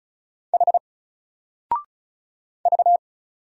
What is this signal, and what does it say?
Morse code, heard as a keyed tone, about 35 wpm, 700 Hz. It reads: H V
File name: Single-Letter-Number-35wpm